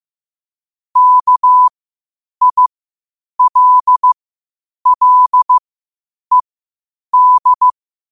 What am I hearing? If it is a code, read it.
Morse code, heard as a keyed tone, about 15 wpm, spaced as for 8 wpm, 1000 Hz. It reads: KILLED
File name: qotdfiles / 8wpm